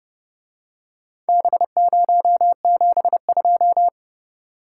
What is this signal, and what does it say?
Morse code, heard as a keyed tone, about 29 wpm, 700 Hz. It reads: B072